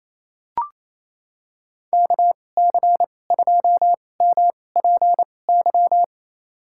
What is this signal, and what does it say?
Morse code, heard as a keyed tone, about 27 wpm, spaced as for 20 wpm, 700 Hz. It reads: KC2MPY
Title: USA Call Signs - 2x Character Spacing - 28wpm